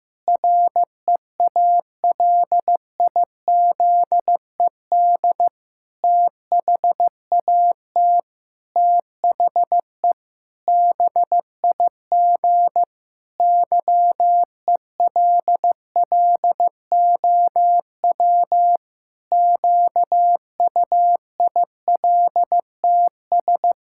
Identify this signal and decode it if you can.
Morse code, heard as a keyed tone, about 15 wpm, 700 Hz. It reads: REALIZED THAT THE BIG YELLOW QUILTS
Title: Pangram Challenge v2 - 15wpm